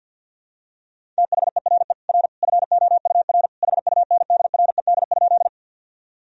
Morse code, heard as a keyed tone, about 51 wpm, 700 Hz. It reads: THERE R FOUR HUNDRED?
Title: Sentences from Top 400 Words and Abbreviations 50wpm